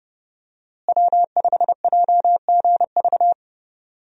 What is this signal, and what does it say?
Morse code, heard as a keyed tone, about 30 wpm, 700 Hz. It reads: W5JGV
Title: USA Call Signs - 5 Characters - Repeated 2x - 30wpm